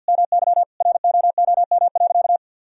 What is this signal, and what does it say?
Morse code, heard as a keyed tone, about 48 wpm, 700 Hz. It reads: MY ROOM.
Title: Sentences from Top 1000 Words - Review Part 1 - 50wpm